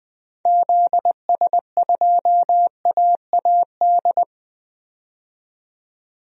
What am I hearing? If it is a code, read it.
Morse code, heard as a keyed tone, about 21 wpm, 700 Hz. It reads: ZS2AAD
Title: DXCC-100 Call Signs - 6 Characters - 20wpm